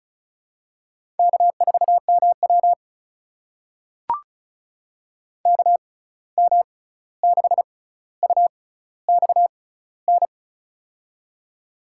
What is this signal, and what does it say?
Morse code, heard as a keyed tone, about 36 wpm, spaced as for 16 wpm, 700 Hz. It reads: K4MW KM6UXN
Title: USA Call Signs - 6x Character Spacing - 35wpm